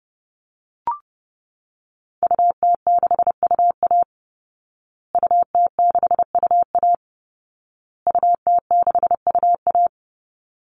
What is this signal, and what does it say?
Morse code, heard as a keyed tone, about 30 wpm, 700 Hz. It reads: UT6UA UT6UA UT6UA
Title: Call Signs Repeated 2x - 30wpm